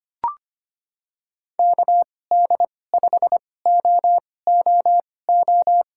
Morse code, heard as a keyed tone, about 25 wpm, spaced as for 18 wpm, 700 Hz. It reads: KD5OOO
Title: USA Call Signs - 6 Characters - 2x Character Spacing - 25wpm